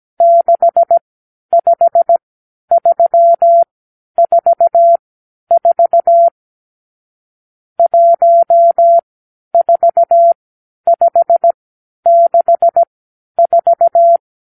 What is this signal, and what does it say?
Morse code, heard as a keyed tone, about 17 wpm, spaced as for 10 wpm, 680 Hz. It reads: 65344 14564
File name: k7qo / Lesson 50 (1-6)